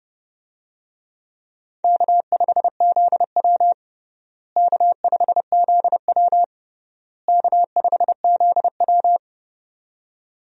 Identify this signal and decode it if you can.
Morse code, heard as a keyed tone, about 32 wpm, 700 Hz. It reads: K5ZW K5ZW K5ZW
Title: USA Call Signs - 4 Characters - Repeated 3x - 30wpm